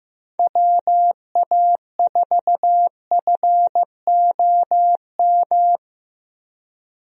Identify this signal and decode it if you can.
Morse code, heard as a keyed tone, about 15 wpm, 700 Hz. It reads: WA4FOM
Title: USA Call Signs - 6 Characters - Repeated 3x - 15wpm